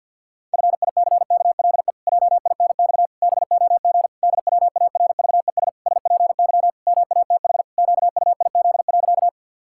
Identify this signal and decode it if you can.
Morse code, heard as a keyed tone, about 50 wpm, 700 Hz. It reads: FICKLE JINX BOG DWARVES SPY MATH QUIZ.
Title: Pangram Challenge v2 - 50wpm